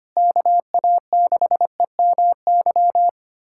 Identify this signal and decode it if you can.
Morse code, heard as a keyed tone, about 25 wpm, 700 Hz. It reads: KA6EMY
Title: USA Callsign DE Callsign - 6 Characters - 25wpm